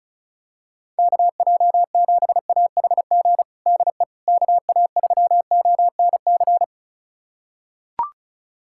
Morse code, heard as a keyed tone, about 34 wpm, 700 Hz. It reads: KJ7AHG DE KA3ONC
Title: USA Callsign DE Callsign 35wpm